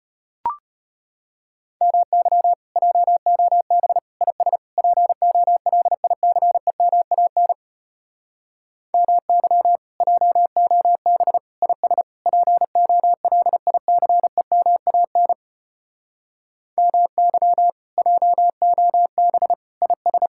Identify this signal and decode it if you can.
Morse code, heard as a keyed tone, about 33 wpm, 700 Hz. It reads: MY JOB IS POLICEMAN MY JOB IS POLICEMAN MY JOB IS